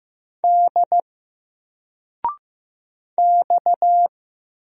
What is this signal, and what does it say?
Morse code, heard as a keyed tone, about 15 wpm, 700 Hz. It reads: D X
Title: Single Letter-Number 15wpm